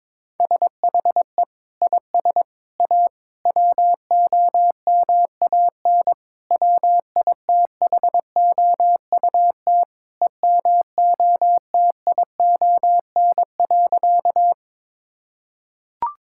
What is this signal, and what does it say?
Morse code, heard as a keyed tone, about 22 wpm, 700 Hz. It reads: SHE IS A WOMAN WITHOUT EMOTION.
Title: Sentences from Top 1600 Words 22wpm